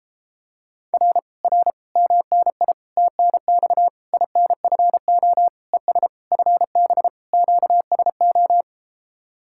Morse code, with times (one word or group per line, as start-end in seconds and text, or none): 0.9-1.2 R
1.4-1.7 R
2.0-2.7 MNI
3.0-3.9 TNX
4.1-5.5 INFO
5.7-6.1 ES
6.3-7.1 FB
7.3-8.6 QSO